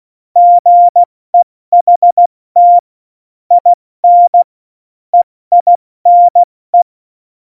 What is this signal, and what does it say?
Morse code, heard as a keyed tone, about 16 wpm, spaced as for 14 wpm, 700 Hz. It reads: GEHT IN EINE
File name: Morsekurs / Witze / Witz 22. Dezember 2020, 70 F80